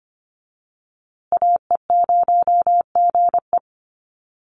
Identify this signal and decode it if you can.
Morse code, heard as a keyed tone, about 25 wpm, 700 Hz. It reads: AE0GE
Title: USA Call Signs Repeated 2x 25wpm